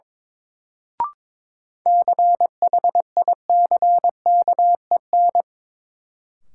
Morse code, heard as a keyed tone, about 22 wpm, 700 Hz. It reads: CHICKEN
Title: Sentences from Top 1000 Words - Review Part 5 - New Word Drill - 22wpm